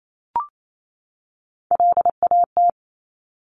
Morse code, heard as a keyed tone, about 28 wpm, 700 Hz. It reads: LAT
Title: Top Three Letter Combinations in Words 28wpm